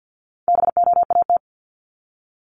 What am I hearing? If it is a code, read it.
Morse code, heard as a keyed tone, about 50 wpm, 700 Hz. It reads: BOAT
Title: Multi-letter Phonemes 50wpm